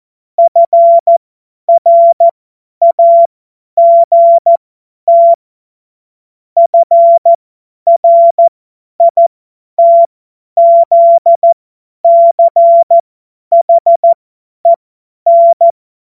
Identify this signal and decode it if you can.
Morse code, heard as a keyed tone, about 14 wpm, spaced as for 10 wpm, 680 Hz. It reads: FRAGT FRITZCHEN
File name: Morsekurs / Witze / Witz 28. September 2021, 50 F70